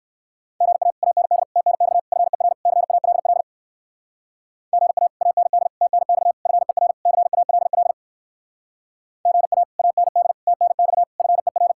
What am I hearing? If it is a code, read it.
Morse code, heard as a keyed tone, about 59 wpm, 700 Hz. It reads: GA AND TNX FER CALL GA AND TNX FER CALL GA AND TNX FER